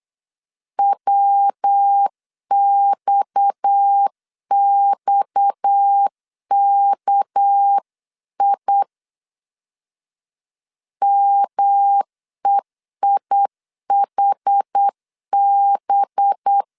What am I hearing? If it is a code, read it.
Morse code, heard as a keyed tone, about 8 wpm, 800 Hz. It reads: WXXKI MEIHB
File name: LEZIONI / 6 PKX0 / 6 PKX 8wpm 3041